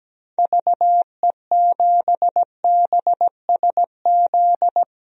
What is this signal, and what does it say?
Morse code, heard as a keyed tone, about 17 wpm, 700 Hz. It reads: VE7BSZ
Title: DXCC-100 Call Signs - 6 Characters - 17wpm